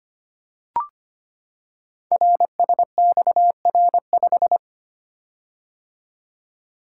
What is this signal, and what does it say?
Morse code, heard as a keyed tone, about 25 wpm, 700 Hz. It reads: RSXR5
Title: Code Groups - 5 Letter:Numbers - 25wpm